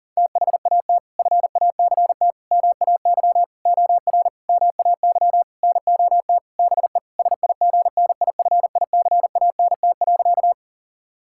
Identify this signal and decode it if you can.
Morse code, heard as a keyed tone, about 40 wpm, 700 Hz. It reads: THAT FACT MAY OR MAY NOT BE SIGNIFICANT.
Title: Sentences from Top 1000 Words - Review Part 6 - 40wpm